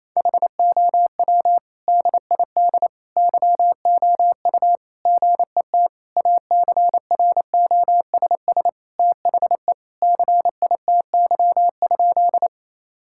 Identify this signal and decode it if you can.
Morse code, heard as a keyed tone, about 28 wpm, 700 Hz. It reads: HOW DID YOU GET ACROSS THE CITY?
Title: Sentences from Top 1000 Words - Review Part 5 - 28wpm